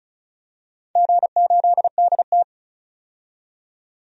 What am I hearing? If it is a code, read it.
Morse code, heard as a keyed tone, about 34 wpm, 700 Hz. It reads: G8DT
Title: DXCC-100 Call Signs - 4 Characters - 35wpm